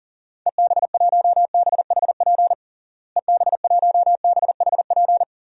Morse code, heard as a keyed tone, about 37 wpm, 700 Hz. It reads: EB1BHP EB1BHP
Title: DXCC-100 Call Signs - 6 Characters - Repeated 2x - 40wpm